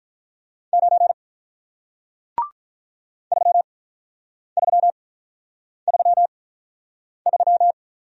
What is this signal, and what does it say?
Morse code, heard as a keyed tone, about 49 wpm, spaced as for 9 wpm, 700 Hz. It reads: 9 3333